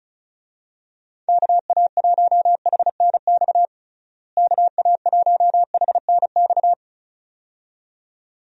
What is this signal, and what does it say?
Morse code, heard as a keyed tone, about 36 wpm, 700 Hz. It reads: KA1HNX KA1HNX
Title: USA Call Signs - 6 Characters - Repeated 2x - 35wpm